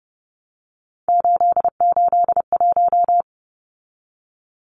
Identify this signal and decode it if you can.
Morse code, heard as a keyed tone, about 30 wpm, 700 Hz. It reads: 881
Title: Numbers - 3-Digit 30wpm